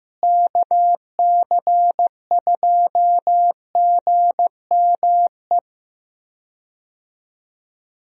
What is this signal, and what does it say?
Morse code, heard as a keyed tone, about 15 wpm, 700 Hz. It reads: KC2GME
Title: USA Call Signs - 6 Characters - Repeated 2x - 15wpm